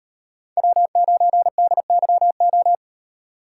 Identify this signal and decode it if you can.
Morse code, heard as a keyed tone, about 38 wpm, 700 Hz. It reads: W9DYO